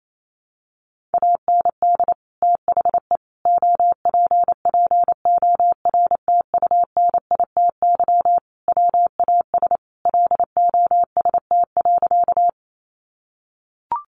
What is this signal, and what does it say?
Morse code, heard as a keyed tone, about 28 wpm, 700 Hz. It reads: AND THE OPPORTUNITY WAS LOST.
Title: Sentences from Top 1400 Words 28wpm